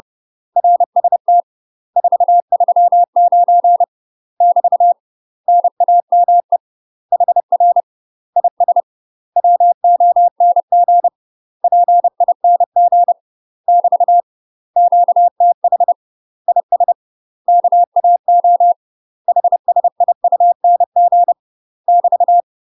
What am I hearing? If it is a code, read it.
Morse code, heard as a keyed tone, about 30 wpm, 700 Hz. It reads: RST 439 <BT> NAME HR IS WONG PING <BT> QTH IS KAO HSIUNG <BT>
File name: UK MASTER QSOs TRAINING / 30WPM / UK MASTER QSOs 0050